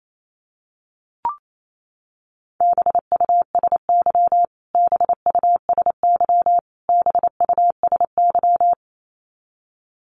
Morse code, heard as a keyed tone, about 28 wpm, 700 Hz. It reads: BUSY BUSY BUSY
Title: Top 500 Words Repeated 28wpm